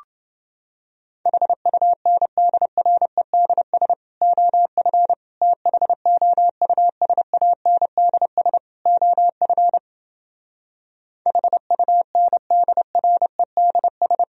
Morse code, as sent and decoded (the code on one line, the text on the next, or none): .... ..- -. -.. .-. -.. ... / --- ..-. / - .... --- ..- ... .- -. -.. ... / --- ..-. / .... ..- -. -.. .-. -.. ...
HUNDREDS OF THOUSANDS OF HUNDREDS